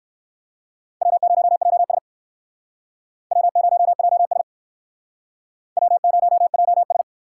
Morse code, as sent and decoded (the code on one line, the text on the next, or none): .-- ----- .--- ... / .-- ----- .--- ... / .-- ----- .--- ...
W0JS W0JS W0JS